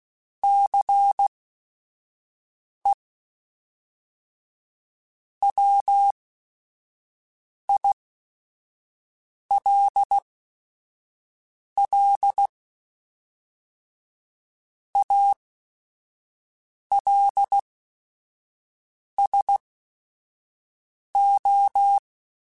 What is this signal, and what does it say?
Morse code, heard as a keyed tone, about 16 wpm, spaced as for 5 wpm, 800 Hz. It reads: CE WILL ALSO
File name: Technical Subjects / Technical Subjects-mp3s / XC @5